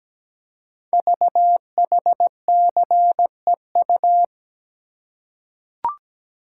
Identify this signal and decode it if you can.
Morse code, heard as a keyed tone, about 17 wpm, 700 Hz. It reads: VHCEU
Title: Code Groups - 5 Letters - 17wpm